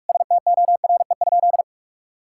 Morse code, heard as a keyed tone, about 46 wpm, 700 Hz. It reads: STORE?
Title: Sentences from Top 1000 Words - Review Part 5 - 45wpm